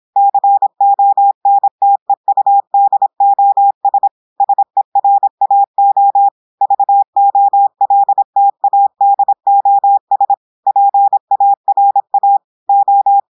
Morse code, heard as a keyed tone, about 26 wpm, 800 Hz. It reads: CONTEUDOS SERAO VOLTADOS PARA O
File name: portugais texte en clair 2 vitesse 26 mots minutes